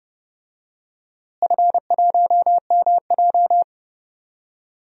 Morse code, heard as a keyed tone, about 30 wpm, 700 Hz. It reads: F1MJ
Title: DXCC-100 Call Signs - 4 Characters - Repeated 2x - 30wpm